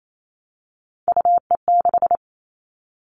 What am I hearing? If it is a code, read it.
Morse code, heard as a keyed tone, about 28 wpm, 700 Hz. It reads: UE6
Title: All Cut Numbers - 3-Digit 28wpm